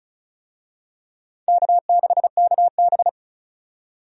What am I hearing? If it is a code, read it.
Morse code, heard as a keyed tone, about 34 wpm, 700 Hz. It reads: K6KB
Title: USA Call Signs - 4 Characters - 35wpm